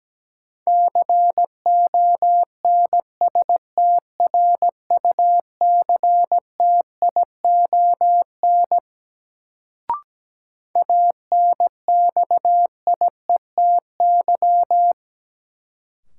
Morse code, heard as a keyed tone, about 17 wpm, 700 Hz. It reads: CONSTRUCTION ANXIETY
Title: Sentences from Top 1600 Words - New Word Drill Alt - 17wpm